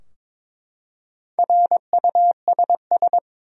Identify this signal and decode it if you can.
Morse code, heard as a keyed tone, about 22 wpm, 700 Hz. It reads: RUSS
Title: USA Common Names - Max 5 Letters - 22wpm